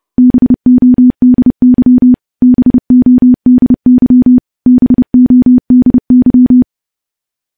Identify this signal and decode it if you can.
Morse code, heard as a keyed tone, about 30 wpm, 250 Hz. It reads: BODY BODY BODY
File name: morse / morse-training-30WPM